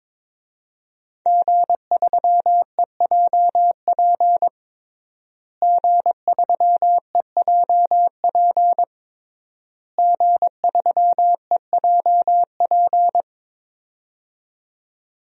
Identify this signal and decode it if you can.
Morse code, heard as a keyed tone, about 23 wpm, 700 Hz. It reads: G3EJP G3EJP G3EJP
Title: DXCC-100 Call Signs - 5 Characters - Repeated 3x - 22wpm